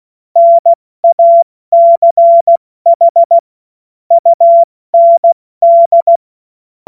Morse code, heard as a keyed tone, about 16 wpm, 680 Hz. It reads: NACH UND
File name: Morsekurs / Witze / Witz 09 Januar 2024 70 F80